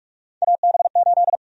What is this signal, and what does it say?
Morse code, heard as a keyed tone, about 46 wpm, 700 Hz. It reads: AB8
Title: USA Call Sign Prefixes - 45wpm